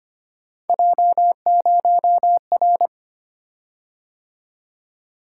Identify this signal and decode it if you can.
Morse code, heard as a keyed tone, about 25 wpm, 700 Hz. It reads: J0R
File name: Three Letter-Number Combinations - 25wpm